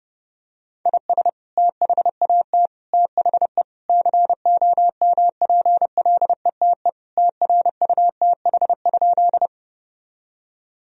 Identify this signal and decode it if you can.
Morse code, heard as a keyed tone, about 30 wpm, 700 Hz. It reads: IS THAT THE COMPLETE TRUTH?
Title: Sentences from Top 1000 Words - Review Part 2 - 30wpm